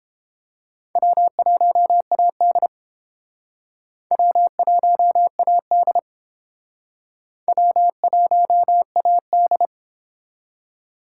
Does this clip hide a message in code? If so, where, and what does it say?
Yes: Morse code from 1.0 to 9.7 s, heard as a keyed tone, about 30 wpm, 700 Hz. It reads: W1AD W1AD W1AD